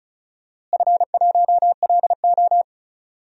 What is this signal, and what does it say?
Morse code, heard as a keyed tone, about 36 wpm, 700 Hz. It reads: F1LO
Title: DXCC-100 Call Signs - 4 Characters - Repeated 2x - 35wpm